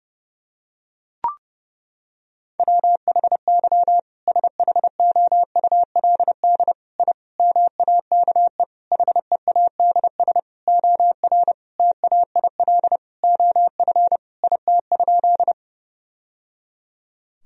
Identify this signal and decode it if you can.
Morse code, heard as a keyed tone, about 32 wpm, 700 Hz. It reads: WHY SHOULD I MAKE HEADS OR TAIL OF IT?
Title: Sentences from Top 1000 Words - Review Part 3 - 30wpm